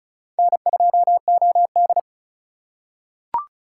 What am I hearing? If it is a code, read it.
Morse code, heard as a keyed tone, about 36 wpm, 700 Hz. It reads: N2OD
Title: USA Call Signs - 3x Character Spacing - 35wpm